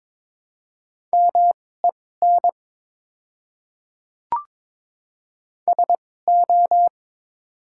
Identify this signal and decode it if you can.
Morse code, heard as a keyed tone, about 22 wpm, spaced as for 16 wpm, 700 Hz. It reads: MEN SO
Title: 2x Character Spacing - Top 200 Words - 22wpm